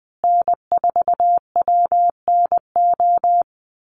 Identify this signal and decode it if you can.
Morse code, heard as a keyed tone, about 20 wpm, 700 Hz. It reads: N4WNO
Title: USA Call Signs Repeated 2x 20wpm